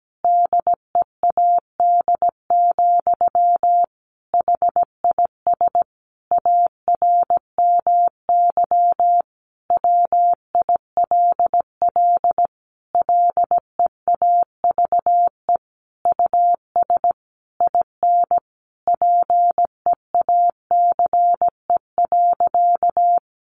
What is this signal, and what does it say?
Morse code, heard as a keyed tone, about 17 wpm, 700 Hz. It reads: DEAD, HIS ARMY WILL LEAVE US IN PEACE.
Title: Sentences from Top 1500 Words 17wpm